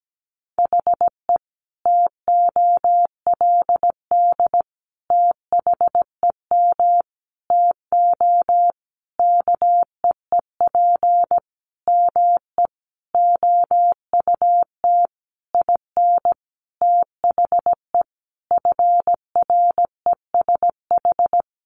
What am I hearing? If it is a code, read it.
Morse code, heard as a keyed tone, about 17 wpm, 700 Hz. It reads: HE TOLD THEM TO KEEP ME OUT IN THE FRESH
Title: Sentences from Top 1200 Words 17wpm